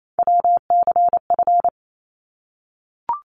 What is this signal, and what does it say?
Morse code, heard as a keyed tone, about 28 wpm, 700 Hz. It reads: WCF
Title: ARRL Sections 28wpm